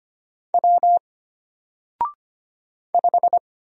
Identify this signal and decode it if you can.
Morse code, heard as a keyed tone, about 25 wpm, 700 Hz. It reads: W 5
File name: Single-Letter-Number-25wpm